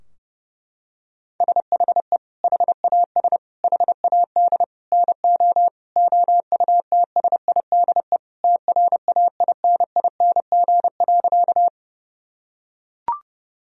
Morse code, heard as a keyed tone, about 30 wpm, 700 Hz. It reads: SHE HAS HAD NO OUTSIDE TRAINING.
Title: Sentences from Top 1000 Words - Review Part 4 - 30wpm